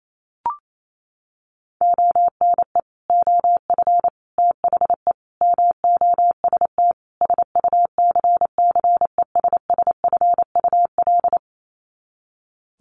Morse code, heard as a keyed tone, about 28 wpm, 700 Hz. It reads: ONE OF THE MOST SUCCESSFUL
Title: Sets of 5 Words Encore2 28wpm